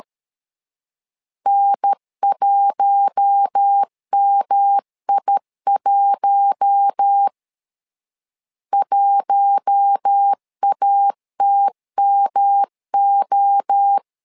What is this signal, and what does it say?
Morse code, heard as a keyed tone, about 13 wpm, 800 Hz. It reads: N1MI1 1ATMO
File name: LEZIONI / 4 Q1 12wpm 2028